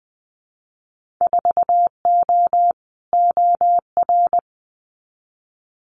Morse code, heard as a keyed tone, about 20 wpm, 700 Hz. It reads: 4O OR
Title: Winter Field Day Exchanges Encore 20wpm